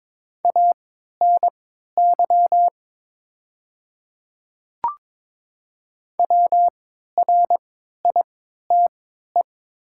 Morse code, heard as a keyed tone, about 22 wpm, spaced as for 12 wpm, 700 Hz. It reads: ANY WRITE
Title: 3x Character Spacing - Top 200 Words - 22wpm